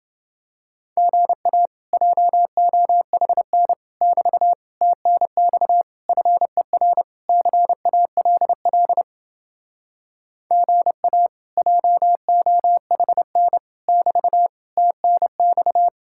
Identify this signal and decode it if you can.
Morse code, heard as a keyed tone, about 29 wpm, 700 Hz. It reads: GA JOHN <BT> TNX FER CALL GA JOHN <BT> TNX